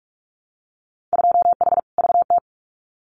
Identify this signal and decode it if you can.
Morse code, heard as a keyed tone, about 44 wpm, 700 Hz. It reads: 2H VT
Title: Winter Field Day Exchanges 45wpm